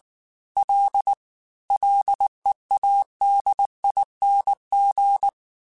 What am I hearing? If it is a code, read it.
Morse code, heard as a keyed tone, about 19 wpm, 800 Hz. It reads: L LEADING
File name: F and L / F and L mp3 / F and L Prefixes 3x2 Twice @18